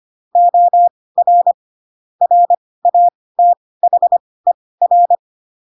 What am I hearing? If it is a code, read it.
Morse code, heard as a keyed tone, about 25 wpm, spaced as for 18 wpm, 700 Hz. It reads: OR RATHER
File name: Pinocchio / Pinocchio ch7 12 25 ik1hge